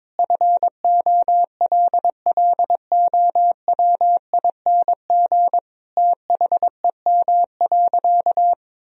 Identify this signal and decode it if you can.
Morse code, heard as a keyed tone, about 21 wpm, 700 Hz. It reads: FOLLOWING THEM.
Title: Sentences from Top 1000 Words - Review Part 4 - 22wpm